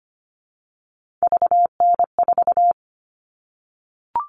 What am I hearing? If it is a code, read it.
Morse code, heard as a keyed tone, about 25 wpm, 700 Hz. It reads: VN4
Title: All Cut Numbers - 3-Digit 25wpm